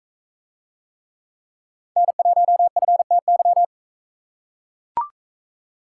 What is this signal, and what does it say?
Morse code, heard as a keyed tone, about 39 wpm, 700 Hz. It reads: N1FTY